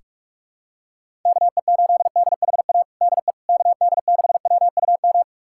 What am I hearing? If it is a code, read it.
Morse code, heard as a keyed tone, about 41 wpm, 700 Hz. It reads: KE8DHA DE KD6WUM
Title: USA Callsign DE Callsign - 6 Characters - 45wpm